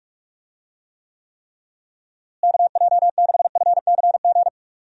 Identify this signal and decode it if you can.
Morse code, heard as a keyed tone, about 42 wpm, 700 Hz. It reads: KJ6FCG
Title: USA Call Signs - 6 Characters - Repeated 3x - 45wpm